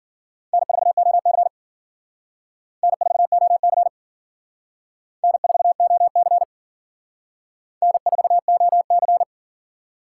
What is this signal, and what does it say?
Morse code, heard as a keyed tone, about 50 wpm, 700 Hz. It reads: N4OC N4OC N4OC N4OC
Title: Speed Racing - USA 1x2 Call Signs 40wpm